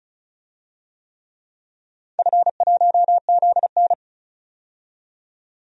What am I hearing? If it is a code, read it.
Morse code, heard as a keyed tone, about 36 wpm, 700 Hz. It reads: F1ZN
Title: DXCC-100 Call Signs - 4 Characters - 35wpm